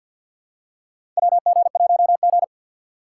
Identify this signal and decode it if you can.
Morse code, heard as a keyed tone, about 48 wpm, 690 Hz. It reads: WG1G
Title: USA Call Signs - 4 Characters - 6x Character Spacing - 50wpm